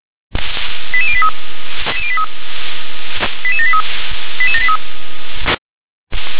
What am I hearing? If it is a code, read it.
Five-tone selective call (ZVEI), heard as quick runs of five tones, tones of about 70 ms. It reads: 88083 88783 88783
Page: ZVEI